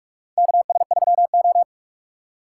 Morse code, heard as a keyed tone, about 46 wpm, 700 Hz. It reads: KS3O
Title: USA Call Signs - 4 Characters - 2x Character Spacing - 45wpm